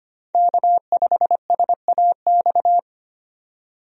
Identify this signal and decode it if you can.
Morse code, heard as a keyed tone, about 25 wpm, 700 Hz. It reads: K5SAX
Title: USA Call Signs - 5 Characters - 25wpm